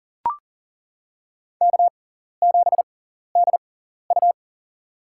Morse code, heard as a keyed tone, about 40 wpm, spaced as for 14 wpm, 700 Hz. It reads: K7DU
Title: USA Call Signs - 6x Character Spacing - 40wpm